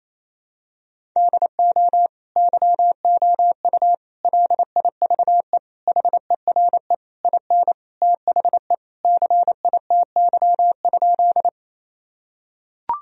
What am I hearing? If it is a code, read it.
Morse code, heard as a keyed tone, about 28 wpm, 700 Hz. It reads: DO YOU LIVE HERE IN THE CITY?
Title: Sentences from Top 1000 Words - Review Part 1 - 28wpm